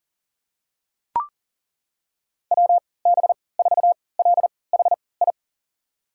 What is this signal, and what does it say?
Morse code, heard as a keyed tone, about 40 wpm, spaced as for 23 wpm, 700 Hz. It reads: WB4LHI
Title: USA Call Signs - 6 Characters - 3x Character Spacing - 40wpm